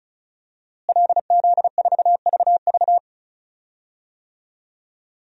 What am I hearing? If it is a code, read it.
Morse code, heard as a keyed tone, about 34 wpm, 700 Hz. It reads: LZ4VV
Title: DXCC-100 Call Signs - 5 Characters - Repeated 2x - 35wpm